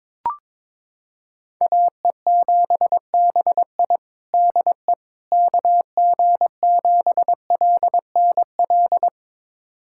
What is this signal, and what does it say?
Morse code, heard as a keyed tone, about 22 wpm, 700 Hz. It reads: AE7BI DE KG7LNL
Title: USA Callsign DE Callsign 22wpm